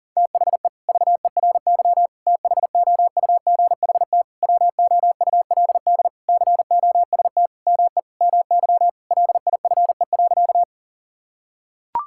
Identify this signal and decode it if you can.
Morse code, heard as a keyed tone, about 37 wpm, 700 Hz. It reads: THE VERY THOUGHT WOULD COST ME MY LIFE.
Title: Sentences from Top 1000 Words - Review Part 5 - 40wpm